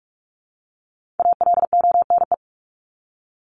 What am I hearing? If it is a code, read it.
Morse code, heard as a keyed tone, about 44 wpm, 700 Hz. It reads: ALONE
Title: Top 751-1000 Words 45wpm